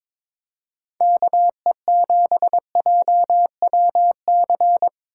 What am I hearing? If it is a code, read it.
Morse code, heard as a keyed tone, about 26 wpm, 700 Hz. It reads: KE7JWC
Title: Speed Racing - USA 2x3 Call Signs 22wpm